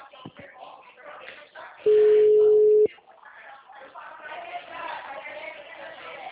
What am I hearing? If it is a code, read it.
Ringback tone (Europe, 425 Hz), with speech.